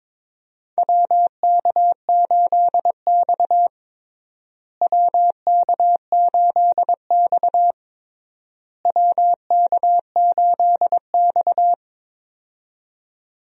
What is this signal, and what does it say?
Morse code, heard as a keyed tone, about 22 wpm, 700 Hz. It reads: WK8X WK8X WK8X
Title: USA Call Signs - 4 Characters - Repeated 3x - 22wpm